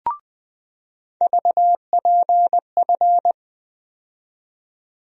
Morse code, heard as a keyed tone, about 20 wpm, 700 Hz. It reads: VPF
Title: USA Call Sign Postfixes - 3 Letters 20wpm